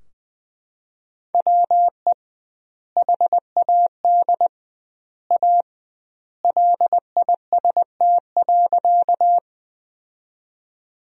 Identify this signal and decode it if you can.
Morse code, heard as a keyed tone, about 20 wpm, 700 Hz. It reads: WE HAD A LIST.
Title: 2x Word-Spacing - Sentences from Top 400 Words - 20wpm